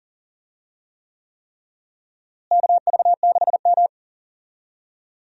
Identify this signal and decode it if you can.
Morse code, heard as a keyed tone, about 37 wpm, 700 Hz. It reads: KV6M
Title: USA Call Signs - 4 Characters - Repeated 3x - 40wpm